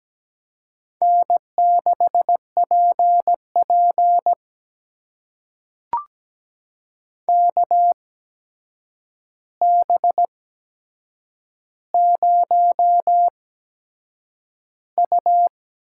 Morse code, heard as a keyed tone, about 17 wpm, spaced as for 7 wpm, 700 Hz. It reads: N6PP KB0U
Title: USA Call Signs - 8x Character Spacing - 17wpm